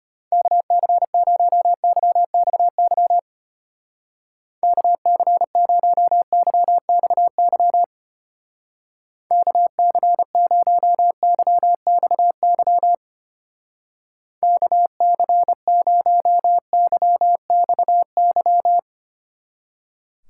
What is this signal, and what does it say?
Morse code, heard as a keyed tone, about 29 wpm, 700 Hz. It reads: KC0YXY KC0YXY KC0YXY KC0YXY
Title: Speed Racing - USA Call Signs 25wpm